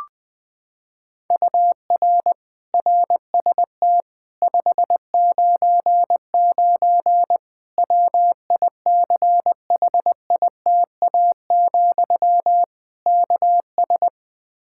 Morse code, heard as a keyed tone, about 20 wpm, 700 Hz. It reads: UR RST 599 WICHITA, KS
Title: Practice QSO - Exchange 1 - Beginning - 20wpm